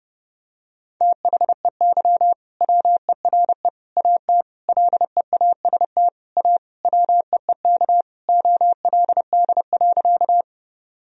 Morse code, heard as a keyed tone, about 32 wpm, 700 Hz. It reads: THEY WERE AT LEAST A WEEK OLD.
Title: Sentences from Top 1000 Words - Review Part 2 - 30wpm